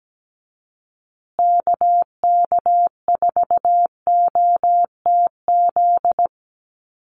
Morse code, heard as a keyed tone, about 17 wpm, 700 Hz. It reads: KK4OTZ
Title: USA Call Signs Repeated 3x 17wpm